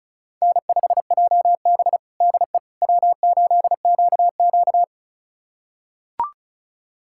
Morse code, heard as a keyed tone, about 36 wpm, 700 Hz. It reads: N5JB DE W8QQ
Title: USA Callsign DE Callsign - 4 Characters - 35wpm